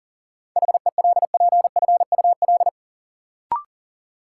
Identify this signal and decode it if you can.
Morse code, heard as a keyed tone, about 40 wpm, 700 Hz. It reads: HELPFUL
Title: Sentences from Top 1000 Words - Review Part 6 - New Word Drill - 40wpm